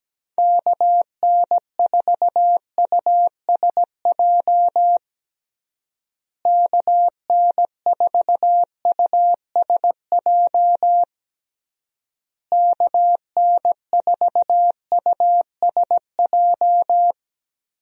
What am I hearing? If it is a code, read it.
Morse code, heard as a keyed tone, about 17 wpm, 700 Hz. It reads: KN4USJ KN4USJ KN4USJ
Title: USA Call Signs - 6 Characters - Repeated 3x - 17wpm